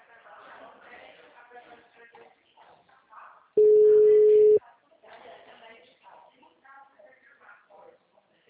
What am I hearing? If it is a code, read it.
Ringback tone (Europe, 425 Hz), with speech.